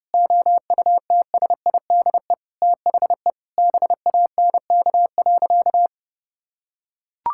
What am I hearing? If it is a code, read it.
Morse code, heard as a keyed tone, about 30 wpm, 700 Hz. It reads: OUTSIDE THE BANK.
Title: Sentences from Top 1000 Words - Review Part 6 - 30wpm